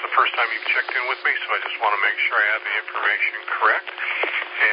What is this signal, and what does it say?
Emergency Alert System (SAME) tones, heard as harsh buzzing bursts, with speech. It is heard from 0.7 to 4.5 s.